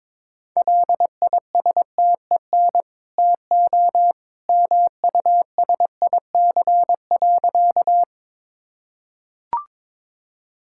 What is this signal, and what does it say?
Morse code, heard as a keyed tone, about 22 wpm, 690 Hz. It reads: LISTEN TO MUSIC.
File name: practice / 22wpm / Easy Sentences with Hints - 22wpm